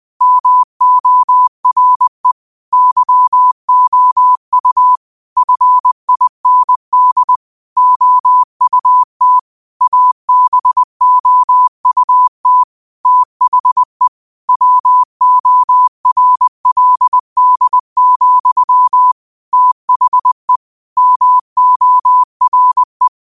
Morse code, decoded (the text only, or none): MORE YOU FIND OUT ABOUT THE WORLD, THE MORE